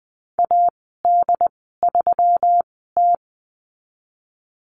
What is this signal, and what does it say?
Morse code, heard as a keyed tone, about 20 wpm, spaced as for 15 wpm, 700 Hz. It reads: AD3T
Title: 122 Lesson 20g - Number 3 - Call Signs 20wpm